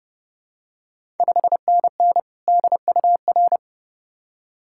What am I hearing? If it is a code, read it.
Morse code, heard as a keyed tone, about 30 wpm, 690 Hz. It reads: 5NN DUR
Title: Ontario QSO Party Exchange 30wpm